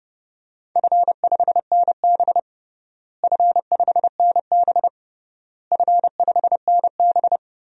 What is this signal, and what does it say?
Morse code, heard as a keyed tone, about 31 wpm, 700 Hz. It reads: F5NB F5NB F5NB
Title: DXCC-100 Call Signs - 4 Characters - Repeated 3x - 30wpm